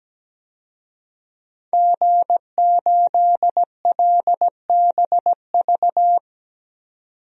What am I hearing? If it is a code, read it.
Morse code, heard as a keyed tone, about 17 wpm, 700 Hz. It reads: G8LBV
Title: DXCC-100 Call Signs - 5 Characters - 17wpm